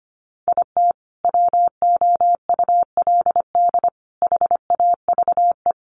Morse code, heard as a keyed tone, about 25 wpm, 700 Hz. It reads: IT WOULD HAVE
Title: Sentences from Top 1100 Words 25wpm